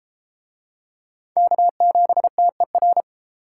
Morse code, heard as a keyed tone, about 31 wpm, 700 Hz. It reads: K7TER